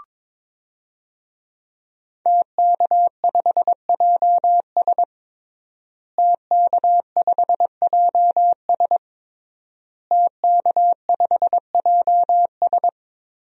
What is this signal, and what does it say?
Morse code, heard as a keyed tone, about 22 wpm, 700 Hz. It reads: TK5JS TK5JS TK5JS
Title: DXCC-100 Call Signs - 5 Characters - Repeated 3x - 22wpm